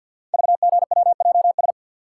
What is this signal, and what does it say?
Morse code, heard as a keyed tone, about 50 wpm, 690 Hz. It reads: VGWJS
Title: Code Groups - 5 Letters - 50wpm